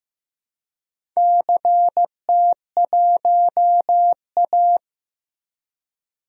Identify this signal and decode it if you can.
Morse code, heard as a keyed tone, about 15 wpm, 700 Hz. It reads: CT1A
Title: DXCC-100 Call Signs - 4 Characters - Repeated 3x - 15wpm